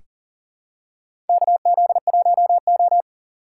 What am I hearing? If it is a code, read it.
Morse code, heard as a keyed tone, about 40 wpm, 700 Hz. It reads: KZ1O
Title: USA Call Signs - 4 Characters - 6x Character Spacing - 40wpm